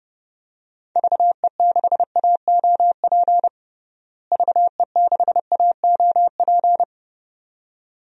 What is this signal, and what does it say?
Morse code, heard as a keyed tone, about 32 wpm, 700 Hz. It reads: VE6AOP VE6AOP
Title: DXCC-100 Call Signs - 6 Characters - Repeated 2x - 30wpm